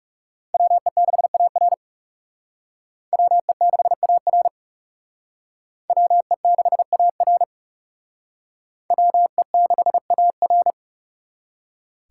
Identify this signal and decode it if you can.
Morse code, heard as a keyed tone, about 35 wpm, 700 Hz. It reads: WE6AR WE6AR WE6AR WE6AR
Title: Speed Racing - USA 2x2 Call Signs 30wpm